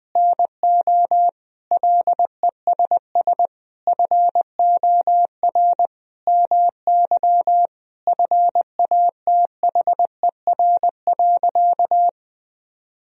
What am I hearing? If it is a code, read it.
Morse code, heard as a keyed tone, about 20 wpm, 700 Hz. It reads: NO LESS FOR MY FATHER.
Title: Sentences from Top 1000 Words - Review Part 2 - 20wpm